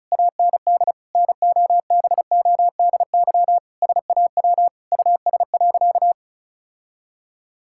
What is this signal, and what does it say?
Morse code, heard as a keyed tone, about 36 wpm, 700 Hz. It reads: AND NOBODY SAW US.
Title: Sentences from Top 1600 Words 35wpm